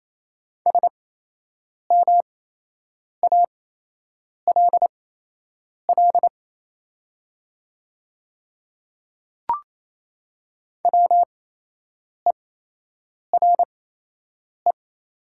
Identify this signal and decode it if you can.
Morse code, heard as a keyed tone, about 28 wpm, spaced as for 8 wpm, 700 Hz. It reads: SMALL WERE
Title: 8x Character Spacing - Top 300 Words - 28wpm